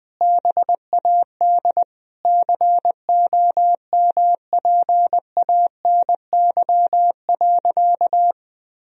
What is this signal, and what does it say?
Morse code, heard as a keyed tone, about 21 wpm, 700 Hz. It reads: BAD COMPANY.
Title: Sentences from Top 1000 Words - Review Part 3 - 20wpm